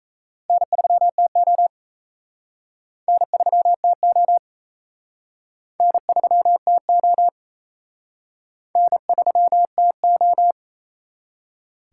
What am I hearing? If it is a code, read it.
Morse code, heard as a keyed tone, about 34 wpm, 700 Hz. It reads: N3TO N3TO N3TO N3TO
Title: Speed Racing - USA Call Signs 28wpm